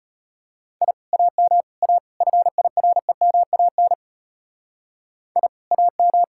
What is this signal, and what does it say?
Morse code, heard as a keyed tone, about 34 wpm, 700 Hz. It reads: I AM A FIREMAN I AM